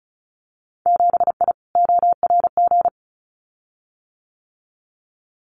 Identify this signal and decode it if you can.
Morse code, heard as a keyed tone, about 34 wpm, 700 Hz. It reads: 7I ORG
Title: Winter Field Day Exchanges Encore 35wpm